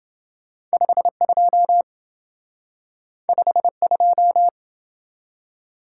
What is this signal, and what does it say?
Morse code, heard as a keyed tone, about 27 wpm, 700 Hz. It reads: 52 52